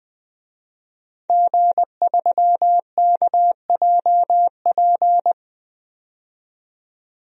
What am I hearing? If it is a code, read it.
Morse code, heard as a keyed tone, about 20 wpm, 700 Hz. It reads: G3KJP
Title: DXCC-100 Call Signs - 5 Characters - 20wpm